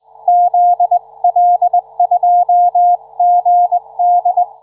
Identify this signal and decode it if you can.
Morse code, heard as a keyed tone, about 20 wpm, 710 Hz. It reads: ZL2GD